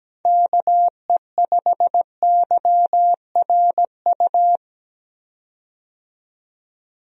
Morse code, heard as a keyed tone, about 17 wpm, 700 Hz. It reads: KE5YRU
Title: USA Call Signs - 6 Characters - 17wpm